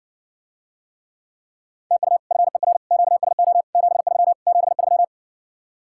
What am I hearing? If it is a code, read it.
Morse code, heard as a keyed tone, about 60 wpm, 700 Hz. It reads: TU FER QSO 73 73